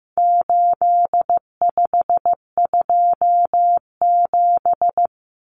Morse code, heard as a keyed tone, about 15 wpm, 700 Hz. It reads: 8527
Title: Common Cut Numbers - 4-Digit 15wpm